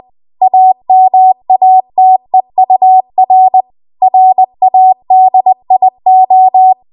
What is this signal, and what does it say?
Morse code, heard as a keyed tone, about 20 wpm, 750 Hz. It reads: AMATEUR RADIO
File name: Morse / 20WPM